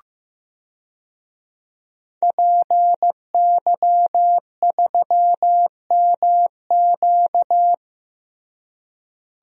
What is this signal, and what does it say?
Morse code, heard as a keyed tone, about 15 wpm, 700 Hz. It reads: PY3MQ
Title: DXCC-100 Call Signs - 5 Characters - Repeated 2x - 15wpm